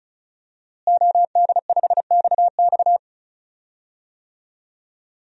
Morse code, heard as a keyed tone, about 36 wpm, 700 Hz. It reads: OD5XX
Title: DXCC-100 Call Signs - 5 Characters - Repeated 2x - 35wpm